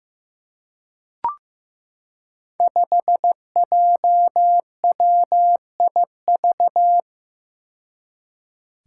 Morse code, heard as a keyed tone, about 15 wpm, 700 Hz. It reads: 5JWIV
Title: Five Challenging Letter-Number Combinations 15wpm